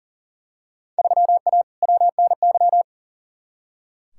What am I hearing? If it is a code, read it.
Morse code, heard as a keyed tone, about 40 wpm, 700 Hz. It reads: 3A WNY
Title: Field Day Exchanges - 40wpm